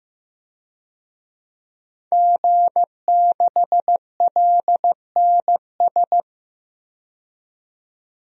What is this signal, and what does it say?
Morse code, heard as a keyed tone, about 15 wpm, 700 Hz. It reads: G6LNS
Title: DXCC-100 Call Signs - 5 Characters - 15wpm